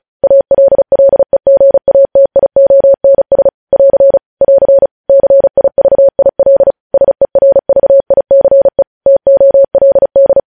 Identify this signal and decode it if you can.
Morse code, heard as a keyed tone, about 36 wpm, 550 Hz. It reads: ALLEGATIONS <AR> <AR> CIVIL SERVICE TOLD
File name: podcast / News headlines-1x-35x35